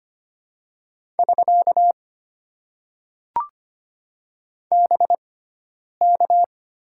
Morse code, heard as a keyed tone, about 25 wpm, spaced as for 9 wpm, 700 Hz. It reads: <SK> BK
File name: Top 90 QSO Elements - 6x Character Spacing - 25wpm